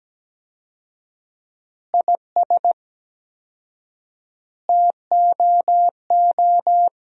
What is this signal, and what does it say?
Morse code, heard as a keyed tone, about 17 wpm, 700 Hz. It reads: IS TOO
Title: 4x Word-Spacing - Sentences from Top 500 Words - 17wpm